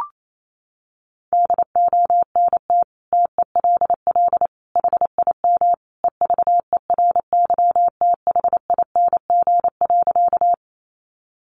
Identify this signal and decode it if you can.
Morse code, heard as a keyed tone, about 28 wpm, 700 Hz. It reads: DONT TELL HIM EVERYTHING.
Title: Sentences from Top 700 Words 28wpm